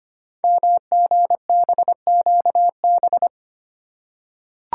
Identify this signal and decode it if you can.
Morse code, heard as a keyed tone, about 25 wpm, 700 Hz. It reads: MGBQB
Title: Code Groups - 5 Letters - 25wpm